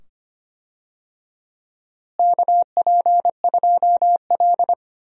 Morse code, heard as a keyed tone, about 25 wpm, 700 Hz. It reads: KP2L
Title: DXCC-100 Call Signs - 4 Characters - Repeated 2x - 25wpm